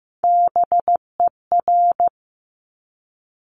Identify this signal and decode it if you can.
Morse code, heard as a keyed tone, about 15 wpm, 700 Hz. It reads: BER
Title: Top Three Letter Combinations in Words 15wpm